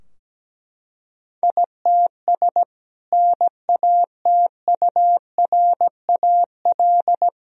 Morse code, heard as a keyed tone, about 17 wpm, 700 Hz. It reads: ITS NATURAL
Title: Sentences from Top 1000 Words - Review Part 4 - 17wpm